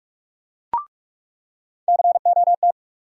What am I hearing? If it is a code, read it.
Morse code, heard as a keyed tone, about 46 wpm, 700 Hz. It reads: COT
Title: USA Call Sign Postfixes - 3 Letters 45wpm